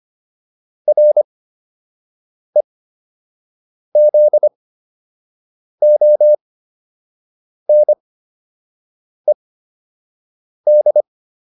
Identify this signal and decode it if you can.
Morse code, heard as a keyed tone, about 25 wpm, spaced as for 6 wpm, 600 Hz. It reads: REZONED